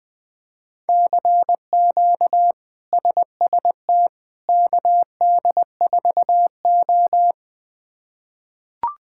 Morse code, heard as a keyed tone, about 20 wpm, 700 Hz. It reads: CQ SST KD4O
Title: K1USN SST QSO Practice - 20wpm